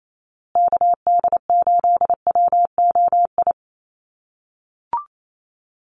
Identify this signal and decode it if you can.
Morse code, heard as a keyed tone, about 28 wpm, 700 Hz. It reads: KD8WOI
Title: USA Call Signs Repeated 2x 28wpm